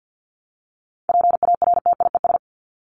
Morse code, heard as a keyed tone, about 50 wpm, 700 Hz. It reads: PARTIES
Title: ES:S Suffix - Top 100 Words with Base Words 50wpm